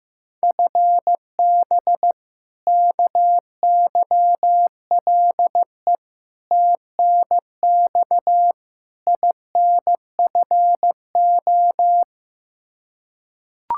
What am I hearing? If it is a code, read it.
Morse code, heard as a keyed tone, about 15 wpm, 700 Hz. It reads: FB KYLE TNX INFO
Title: Practice QSO - Exchange 3 - More Info - 15wpm